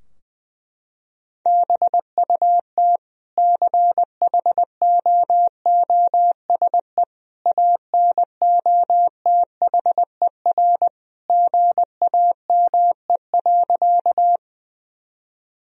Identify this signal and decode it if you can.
Morse code, heard as a keyed tone, about 20 wpm, 700 Hz. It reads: BUT CHOOSE ANOTHER GAME.
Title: Sentences from Top 1000 Words - Review Part 4 - 20wpm